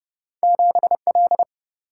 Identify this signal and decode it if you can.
Morse code, heard as a keyed tone, about 30 wpm, 700 Hz. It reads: 7 L
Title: Single-Letter Single-Number 30wpm